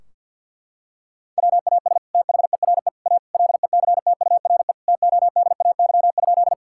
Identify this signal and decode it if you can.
Morse code, heard as a keyed tone, about 50 wpm, 700 Hz. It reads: WAS THERE A LECTURE TODAY?
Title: Sentences from Top 1600 Words 50wpm